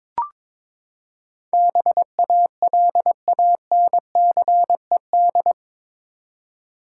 Morse code, heard as a keyed tone, about 22 wpm, 700 Hz. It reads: BALANCED
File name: practice / 22wpm / Eight Letter Words - 22wpm